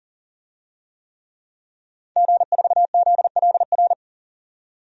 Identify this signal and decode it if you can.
Morse code, heard as a keyed tone, about 40 wpm, 700 Hz. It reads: G4ZLR
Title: DXCC-100 Call Signs - 5 Characters - 40wpm